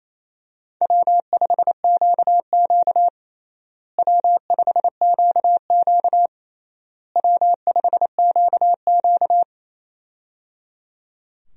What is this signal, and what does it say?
Morse code, heard as a keyed tone, about 27 wpm, 700 Hz. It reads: W5QQ W5QQ W5QQ
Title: USA Call Signs - 4 Characters - Repeated 3x - 28wpm